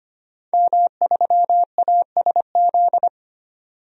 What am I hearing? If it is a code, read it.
Morse code, heard as a keyed tone, about 25 wpm, 700 Hz. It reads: M3ASZ
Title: DXCC-100 Call Signs - 5 Characters - 25wpm